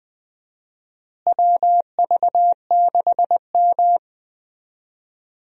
Morse code, heard as a keyed tone, about 20 wpm, 700 Hz. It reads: WV6M
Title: USA Call Signs - 4 Characters - Repeated 3x - 20wpm